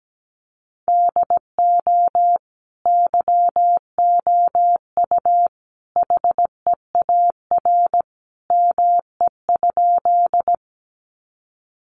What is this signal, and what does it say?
Morse code, heard as a keyed tone, about 17 wpm, 700 Hz. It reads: DO YOU HEAR ME?
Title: Sentences from Top 300 Words 17wpm